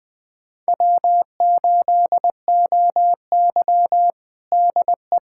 Morse code, heard as a keyed tone, about 21 wpm, 700 Hz. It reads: W8OY DE
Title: USA Callsign DE Callsign - 4 Characters - 20wpm